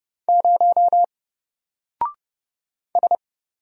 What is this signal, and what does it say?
Morse code, heard as a keyed tone, about 30 wpm, 700 Hz. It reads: 0 S
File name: Single-Letter-Number-30wpm